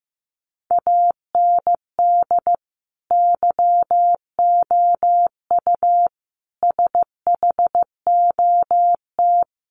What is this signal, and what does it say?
Morse code, heard as a keyed tone, about 15 wpm, 700 Hz. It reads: AND YOU SHOT
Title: Sentences from Top 1500 Words 15wpm